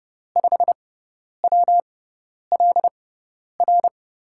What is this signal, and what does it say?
Morse code, heard as a keyed tone, about 30 wpm, spaced as for 10 wpm, 700 Hz. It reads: 5WLR
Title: USA Call Signs - 5 Characters - 6x Character Spacing - 30wpm